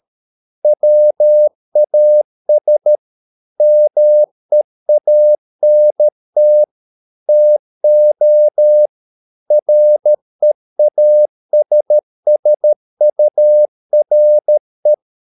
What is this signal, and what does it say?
Morse code, heard as a keyed tone, about 13 wpm, 600 Hz. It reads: WAS MEANT TO REASSURE